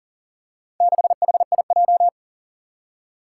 Morse code, heard as a keyed tone, about 38 wpm, 700 Hz. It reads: 6HIJ
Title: Four Challenging Letter-Number Combinations 40wpm